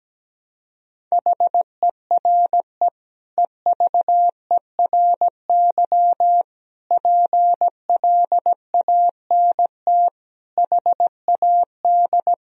Morse code, heard as a keyed tone, about 17 wpm, 700 Hz. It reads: HERE EVERY PLANT HAD
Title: Sentences from Top 1000 Words - Review Part 1 - 17wpm